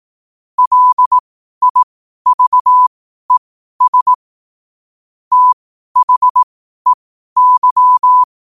Morse code, heard as a keyed tone, about 18 wpm, spaced as for 12 wpm, 1000 Hz. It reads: LIVES THEY
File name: qotdfiles / 12wpm